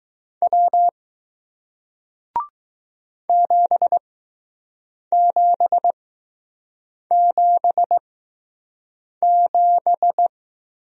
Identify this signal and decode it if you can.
Morse code, heard as a keyed tone, about 20 wpm, spaced as for 6 wpm, 700 Hz. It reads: W 7777